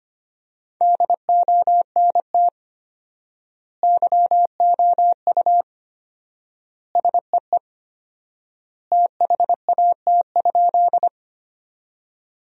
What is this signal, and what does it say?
Morse code, heard as a keyed tone, about 25 wpm, 700 Hz. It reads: DONT YOU SEE THAT?
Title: 4x Word-Spacing - Sentences from Top 300 Words - 25wpm